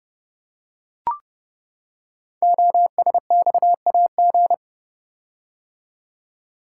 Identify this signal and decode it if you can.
Morse code, heard as a keyed tone, about 30 wpm, 700 Hz. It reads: OSXAG
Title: Code Groups - 5 Letters - 30wpm